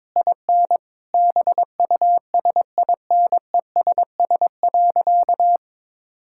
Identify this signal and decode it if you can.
Morse code, heard as a keyed tone, about 22 wpm, 700 Hz. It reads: IN BUSINESS.
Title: Sentences from Top 1000 Words - Review Part 6 - 22wpm